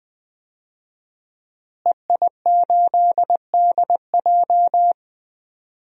Morse code, heard as a keyed tone, about 20 wpm, 700 Hz. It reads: EI8DJ
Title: DXCC-100 Call Signs - 5 Characters - Repeated 2x - 20wpm